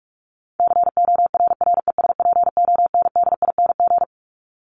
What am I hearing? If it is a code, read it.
Morse code, heard as a keyed tone, about 44 wpm, 700 Hz. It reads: CORRESPONDING
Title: ING Suffix - Top 100 Words with Base Words 45wpm